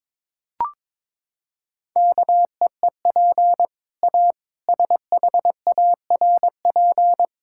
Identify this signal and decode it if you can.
Morse code, heard as a keyed tone, about 22 wpm, 700 Hz. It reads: KEEP A SHARP
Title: Sentences from Top 1600 Words 22wpm